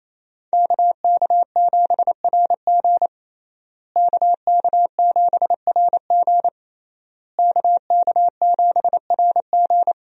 Morse code, heard as a keyed tone, about 28 wpm, 700 Hz. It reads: KK7RG KK7RG KK7RG
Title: USA Call Signs - 5 Characters - Repeated 3x - 28wpm